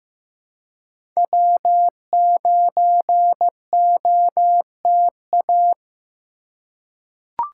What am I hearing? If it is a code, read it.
Morse code, heard as a keyed tone, about 15 wpm, 700 Hz. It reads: W9OTA
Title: USA Call Signs - 2x Character Spacing - 15wpm